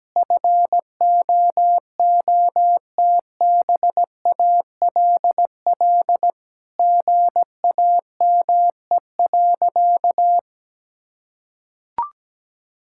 Morse code, heard as a keyed tone, about 17 wpm, 700 Hz. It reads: FOOTBALL GAME.
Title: Sentences from Top 1600 Words 17wpm